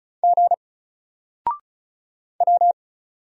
Morse code, heard as a keyed tone, about 36 wpm, 700 Hz. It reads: G W
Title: Single Letter-Number Alt 35wpm